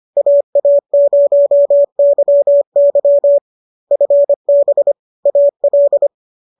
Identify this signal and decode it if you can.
Morse code, heard as a keyed tone, about 25 wpm, 590 Hz. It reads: AA0YY FB AL
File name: qso213 25